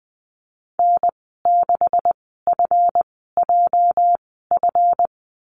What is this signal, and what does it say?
Morse code, heard as a keyed tone, about 20 wpm, spaced as for 15 wpm, 700 Hz. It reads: N6FJF
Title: 231 Lesson 37g - Letter x - Call Signs 20wpm